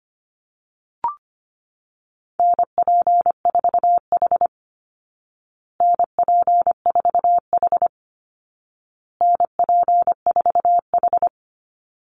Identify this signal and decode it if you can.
Morse code, heard as a keyed tone, about 25 wpm, 700 Hz. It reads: NP4H NP4H NP4H
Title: Call Signs Repeated 2x - 25wpm